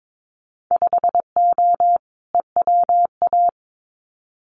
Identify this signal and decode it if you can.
Morse code, heard as a keyed tone, about 22 wpm, 700 Hz. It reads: 5O EWA
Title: Winter Field Day Exchanges Encore 22wpm